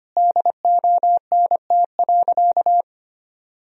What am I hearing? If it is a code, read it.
Morse code, heard as a keyed tone, about 25 wpm, 700 Hz. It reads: DONT.
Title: Sentences from Top 1000 Words - Review Part 4 - 25wpm